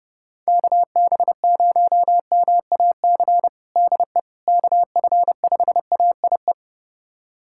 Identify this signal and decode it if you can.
Morse code, heard as a keyed tone, about 30 wpm, 700 Hz. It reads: KB0MAC DE KF5AIE
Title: USA Callsign DE Callsign - 6 Characters - 30wpm